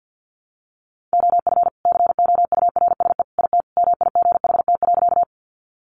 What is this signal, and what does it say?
Morse code, heard as a keyed tone, about 50 wpm, 700 Hz. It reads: OF COURSE IT MIGHT.
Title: Sentences from Top 500 Words 50wpm